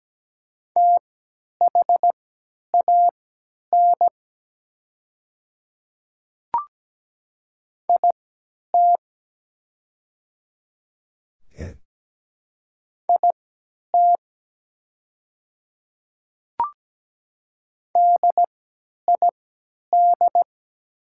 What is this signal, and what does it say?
Morse code, heard as a keyed tone, about 17 wpm, spaced as for 10 wpm, 700 Hz. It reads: THAN IT IT DID
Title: 3x Character Spacing - Top 100 Words - 17wpm